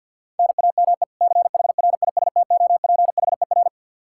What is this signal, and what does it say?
Morse code, heard as a keyed tone, about 51 wpm, 700 Hz. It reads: NAME CHRISTOPHER
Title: Practice QSO - Exchange 1 - Beginning - 50wpm